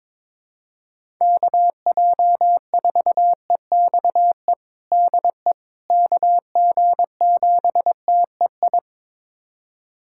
Morse code, heard as a keyed tone, about 22 wpm, 700 Hz. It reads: KJ4EXE DE KG7TEI
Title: USA Callsign DE Callsign - 6 Characters - 22wpm